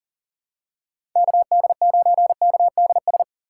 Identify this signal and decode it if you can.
Morse code, heard as a keyed tone, about 40 wpm, 700 Hz. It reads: KD9KDS
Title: USA Call Signs - 6 Characters - 6x Character Spacing - 40wpm